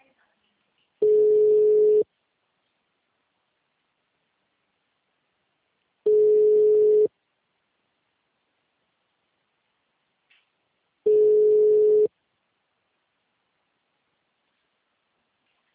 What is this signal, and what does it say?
Ringback tone (Europe, 425 Hz).
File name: out- -2004- - - .8